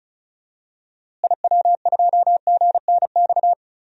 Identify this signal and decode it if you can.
Morse code, heard as a keyed tone, about 36 wpm, 700 Hz. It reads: IW2GNX
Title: DXCC-100 Call Signs - 6 Characters - Repeated 3x - 35wpm